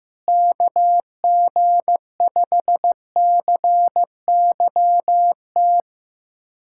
Morse code, heard as a keyed tone, about 15 wpm, 700 Hz. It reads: KG5CYT
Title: USA Callsign DE Callsign - 6 Characters - 15wpm